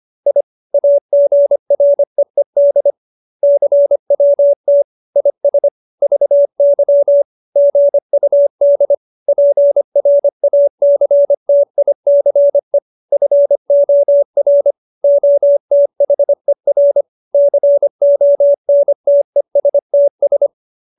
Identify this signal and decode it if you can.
Morse code, heard as a keyed tone, about 25 wpm, 570 Hz. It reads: I AGREED CWT IS VY GUD PRACTICE FOR OTHER CONTESTS